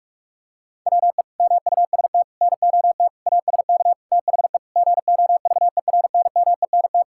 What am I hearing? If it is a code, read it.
Morse code, heard as a keyed tone, about 46 wpm, 700 Hz. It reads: WE MUST NOT ASK THE GOVERNMENT
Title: Sentences from Top 1600 Words 45wpm